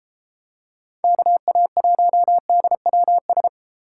Morse code, heard as a keyed tone, about 30 wpm, 700 Hz. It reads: KA1DWS